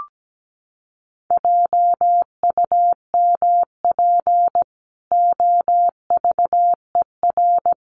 Morse code, heard as a keyed tone, about 17 wpm, 700 Hz. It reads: JUMP OVER
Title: Sentences from Top 1300 Words 17wpm